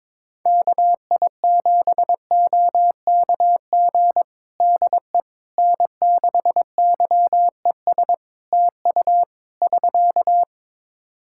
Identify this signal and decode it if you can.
Morse code, heard as a keyed tone, about 22 wpm, 700 Hz. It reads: KI7OKG DE N6YES TU <SK>
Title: Practice QSO - Exchange 4 - Ending - 22wpm